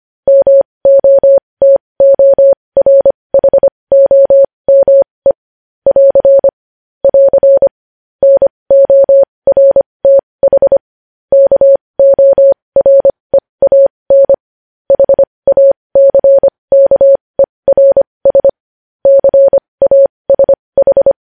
Morse code, heard as a keyed tone, about 25 wpm, spaced as for 20 wpm, 550 Hz. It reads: MOTORHOME <AR> <AR> NORTH KOREAN HACKERS CASH
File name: podcast / News headlines-1x-25x20